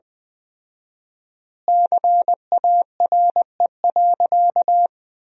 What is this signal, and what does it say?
Morse code, heard as a keyed tone, about 20 wpm, 700 Hz. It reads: CARE.
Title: 4x Word-Spacing - Sentences from Top 300 Words - 20wpm